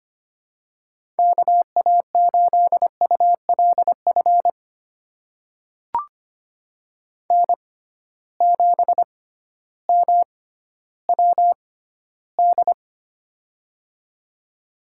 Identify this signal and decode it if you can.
Morse code, heard as a keyed tone, about 25 wpm, spaced as for 13 wpm, 700 Hz. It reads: KA8ULF N7MWD
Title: USA Call Signs - 6x Character Spacing - 25wpm